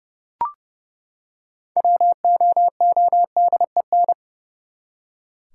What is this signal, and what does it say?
Morse code, heard as a keyed tone, about 30 wpm, 700 Hz. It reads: WOODEN
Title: Six Letter Words - 30wpm